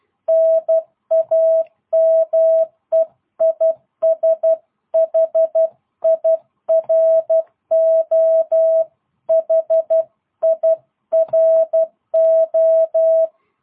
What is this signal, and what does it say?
Morse code, heard as a keyed tone, about 12 wpm, 650 Hz. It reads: NAME IS HIRO HIRO
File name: jh8cbh / 2022cw